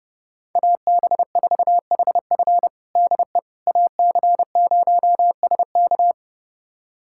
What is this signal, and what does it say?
Morse code, heard as a keyed tone, about 32 wpm, 700 Hz. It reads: AB4HF DE AC0SK
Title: USA Callsign DE Callsign - 5 Characters - 30wpm